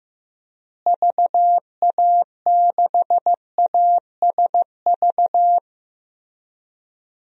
Morse code, heard as a keyed tone, about 15 wpm, 700 Hz. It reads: VA6ASV
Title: DXCC-100 Call Signs - 6 Characters - Repeated 2x - 15wpm